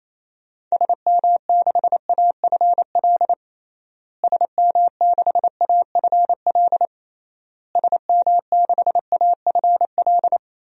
Morse code, heard as a keyed tone, about 28 wpm, 700 Hz. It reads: SM6AFL SM6AFL SM6AFL
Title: DXCC-100 Call Signs - 6 Characters - Repeated 3x - 28wpm